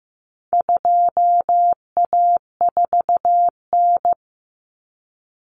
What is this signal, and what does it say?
Morse code, heard as a keyed tone, about 15 wpm, 700 Hz. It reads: 2A4N
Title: Common Cut Numbers - 4-Digit 15wpm